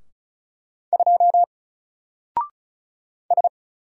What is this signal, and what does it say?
Morse code, heard as a keyed tone, about 34 wpm, 700 Hz. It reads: 2 S
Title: Single Character (Letter:Number:Punctuation:Prosign) 35wpm